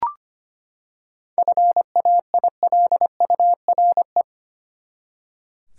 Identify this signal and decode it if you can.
Morse code, heard as a keyed tone, about 25 wpm, 700 Hz. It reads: FAILURE
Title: Seven Letter Words - 25wpm